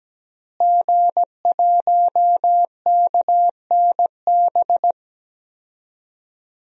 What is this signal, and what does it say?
Morse code, heard as a keyed tone, about 17 wpm, 700 Hz. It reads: G1KNB
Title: DXCC-100 Call Signs - 5 Characters - Repeated 2x - 17wpm